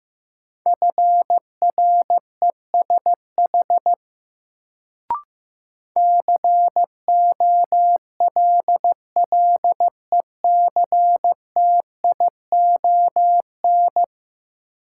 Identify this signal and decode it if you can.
Morse code, heard as a keyed tone, about 15 wpm, 700 Hz. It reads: FRESH COLLECTION
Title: Sentences from Top 1200 Words - New Word Drill Alt - 15wpm